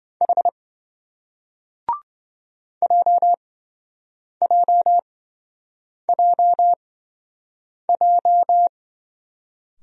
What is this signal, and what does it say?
Morse code, heard as a keyed tone, about 27 wpm, spaced as for 7 wpm, 700 Hz. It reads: H JJJJ